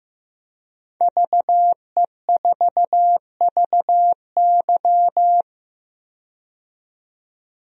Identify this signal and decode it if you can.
Morse code, heard as a keyed tone, about 15 wpm, 700 Hz. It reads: VE4VY
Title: DXCC-100 Call Signs - 5 Characters - 15wpm